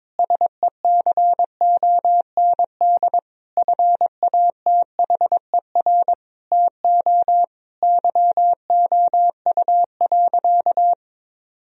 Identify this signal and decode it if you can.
Morse code, heard as a keyed tone, about 22 wpm, 700 Hz. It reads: SECOND FATHER TO YOU.
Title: Sentences from Top 1000 Words - Review Part 1 - 22wpm